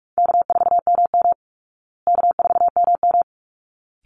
Morse code, heard as a keyed tone, about 44 wpm, 700 Hz. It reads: K4MM K4MM
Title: Call Signs Repeated 3x - 45wpm